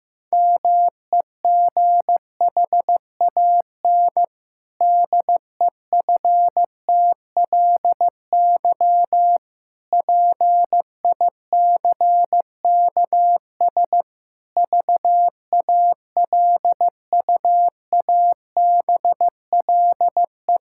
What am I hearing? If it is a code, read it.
Morse code, heard as a keyed tone, about 15 wpm, 700 Hz. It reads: MEGHAN DEFTLY PICKS VALUABLE
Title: Pangram Challenge v2 - 15wpm